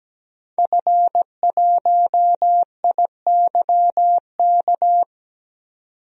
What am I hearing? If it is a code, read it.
Morse code, heard as a keyed tone, about 17 wpm, 700 Hz. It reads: F1IYK
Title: DXCC-100 Call Signs - 5 Characters - 17wpm